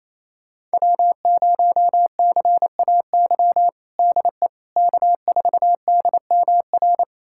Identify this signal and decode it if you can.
Morse code, heard as a keyed tone, about 28 wpm, 700 Hz. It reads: W0CAY DE K4DMR
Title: USA Callsign DE Callsign - 5 Characters - 28wpm